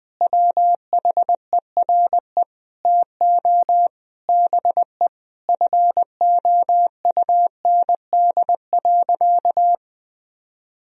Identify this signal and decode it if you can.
Morse code, heard as a keyed tone, about 19 wpm, 700 Hz. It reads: WHERE TO BE FOUND.
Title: Sentences from Top 1000 Words - Review Part 3 - 20wpm